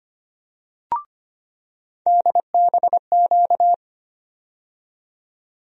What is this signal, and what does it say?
Morse code, heard as a keyed tone, about 25 wpm, 700 Hz. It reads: DBQ
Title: USA Call Sign Postfixes - 3 Letters 25wpm